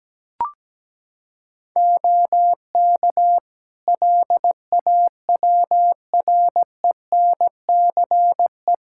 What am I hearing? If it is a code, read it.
Morse code, heard as a keyed tone, about 17 wpm, 700 Hz. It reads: OK LAWRENCE
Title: Practice QSO - Exchange 3 - More Info - 17wpm